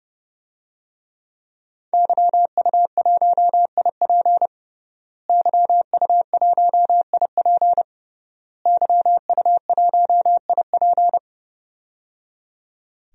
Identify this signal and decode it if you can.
Morse code, heard as a keyed tone, about 30 wpm, 700 Hz. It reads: YU1IP YU1IP YU1IP
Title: DXCC-100 Call Signs - 5 Characters - Repeated 3x - 30wpm